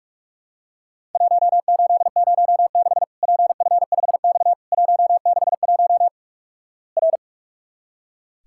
Morse code, heard as a keyed tone, about 44 wpm, 700 Hz. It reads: 1806 PF5X 161 R
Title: Worked All Europe DX Contest - QTC Exchange - 45wpm